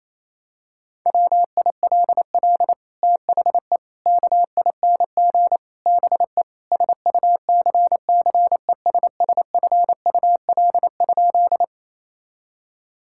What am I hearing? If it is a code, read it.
Morse code, heard as a keyed tone, about 28 wpm, 700 Hz. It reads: WILL THE KING BE SUCCESSFUL?
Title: Sentences from Top 1000 Words - Review Part 6 - 28wpm